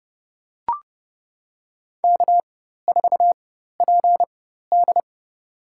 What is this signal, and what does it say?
Morse code, heard as a keyed tone, about 30 wpm, spaced as for 14 wpm, 700 Hz. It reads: K4PD
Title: USA Call Signs - 4 Characters - 4x Character Spacing - 30wpm